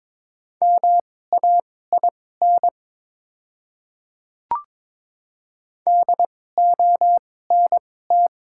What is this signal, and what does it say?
Morse code, heard as a keyed tone, about 22 wpm, spaced as for 16 wpm, 700 Hz. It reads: MAIN DONT
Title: 2x Character Spacing - Top 500 Words - 22wpm